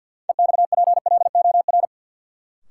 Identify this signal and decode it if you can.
Morse code, heard as a keyed tone, about 52 wpm, 700 Hz. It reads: EXPLOR
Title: Sentences from Top 1100 Words - New Word Drill Alt - 50wpm